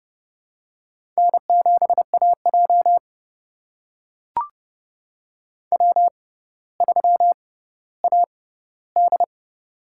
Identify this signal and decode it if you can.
Morse code, heard as a keyed tone, about 30 wpm, spaced as for 15 wpm, 700 Hz. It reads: N7AJ W3AD
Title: USA Call Signs - 6x Character Spacing - 30wpm